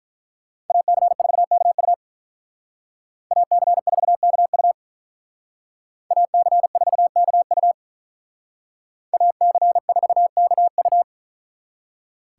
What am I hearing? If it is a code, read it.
Morse code, heard as a keyed tone, about 44 wpm, 700 Hz. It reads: AC4KU AC4KU AC4KU AC4KU